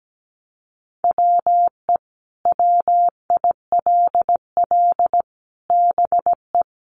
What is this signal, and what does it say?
Morse code, heard as a keyed tone, about 17 wpm, 700 Hz. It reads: WE WILL BE
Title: Sets of 5 Words Encore 17wpm